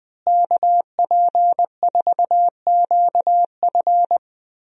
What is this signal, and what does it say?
Morse code, heard as a keyed tone, about 20 wpm, 700 Hz. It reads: KP4QF
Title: DXCC-100 Call Signs - 5 Characters - Repeated 2x - 20wpm